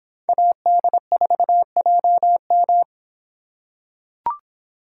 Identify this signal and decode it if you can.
Morse code, heard as a keyed tone, about 24 wpm, 700 Hz. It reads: AD4JM